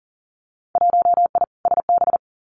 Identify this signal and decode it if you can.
Morse code, heard as a keyed tone, about 40 wpm, 700 Hz. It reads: 1I SB
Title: Winter Field Day Exchanges Encore 40wpm